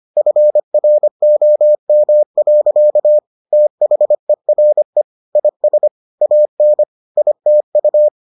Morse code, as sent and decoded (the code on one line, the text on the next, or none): ..-. .-. --- -- .-.-.- / - .... .-. / .. ... / .- -. / .. - ..-
FROM. THERE IS AN ITU